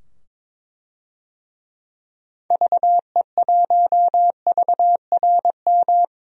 Morse code, heard as a keyed tone, about 21 wpm, 700 Hz. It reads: VE1VRM
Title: DXCC-100 Call Signs - 6 Characters - 22wpm